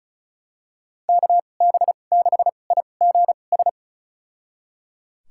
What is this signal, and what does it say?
Morse code, heard as a keyed tone, about 34 wpm, spaced as for 25 wpm, 700 Hz. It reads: KB6IGS
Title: USA Call Signs - 6 Characters - 2x Character Spacing - 35wpm